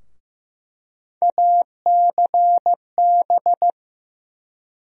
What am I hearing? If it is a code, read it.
Morse code, heard as a keyed tone, about 15 wpm, 700 Hz. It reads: ACB
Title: USA Call Sign Postfixes - 3 Letters 15wpm